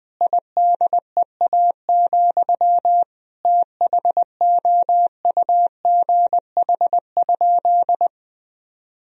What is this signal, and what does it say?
Morse code, heard as a keyed tone, about 20 wpm, 700 Hz. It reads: IDEA, THOUGH?
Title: Sentences from Top 1000 Words - Review Part 2 - 20wpm